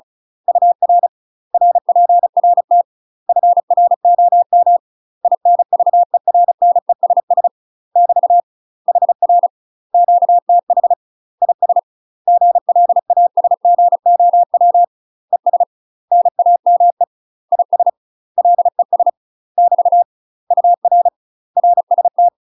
Morse code, heard as a keyed tone, about 36 wpm, 700 Hz. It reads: UR RPRT FROM INVERNESS <BT> HR QTH IS GLASGOW ES NAME IS LES <BT> UR RST
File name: UK MASTER QSOs TRAINING / 35WPM / UK MASTER QSOs 0033